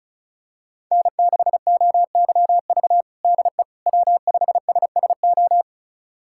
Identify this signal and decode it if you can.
Morse code, heard as a keyed tone, about 36 wpm, 700 Hz. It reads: N6OYV DE W5SSO
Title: USA Callsign DE Callsign - 5 Characters - 35wpm